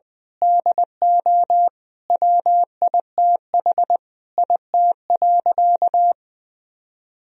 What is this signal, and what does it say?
Morse code, heard as a keyed tone, about 20 wpm, 700 Hz. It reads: DO WITH IT.
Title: Sentences from Top 1000 Words - Review Part 2 - 20wpm